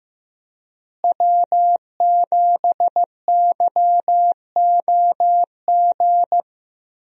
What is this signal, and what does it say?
Morse code, heard as a keyed tone, about 15 wpm, 700 Hz. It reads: W7YOG
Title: USA Call Signs - 5 Characters - Repeated 3x - 15wpm